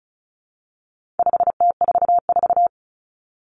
Morse code, heard as a keyed tone, about 34 wpm, 700 Hz. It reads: 5T44
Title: Common Cut Numbers - 4-Digit 35wpm